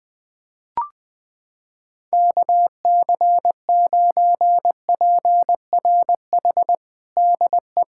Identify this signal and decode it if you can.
Morse code, heard as a keyed tone, about 20 wpm, 700 Hz. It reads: KC9PRH DE
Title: USA Callsign DE Callsign - 6 Characters - 20wpm